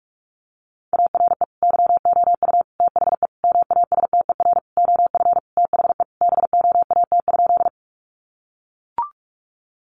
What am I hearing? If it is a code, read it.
Morse code, heard as a keyed tone, about 44 wpm, 700 Hz. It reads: ARE YOU THE MASTER OF THE BOAT?
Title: Sentences from Top 1300 Words 45wpm